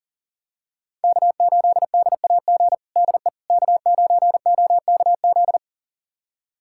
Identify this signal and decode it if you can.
Morse code, heard as a keyed tone, about 40 wpm, 700 Hz. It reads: K8DAG DE K9OKZ
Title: USA Callsign DE Callsign - 5 Characters - 40wpm